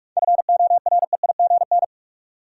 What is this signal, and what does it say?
Morse code, heard as a keyed tone, about 46 wpm, 700 Hz. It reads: FOREIGN
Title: Sentences from Top 1400 Words - New Word Drill - 45wpm